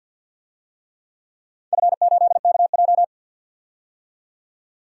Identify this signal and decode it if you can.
Morse code, heard as a keyed tone, about 52 wpm, 700 Hz. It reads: F8KJ
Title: DXCC-100 Call Signs - 4 Characters - 50wpm